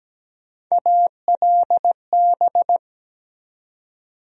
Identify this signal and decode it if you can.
Morse code, heard as a keyed tone, about 17 wpm, 700 Hz. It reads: ALB
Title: USA Call Sign Postfixes - 3 Letters 17wpm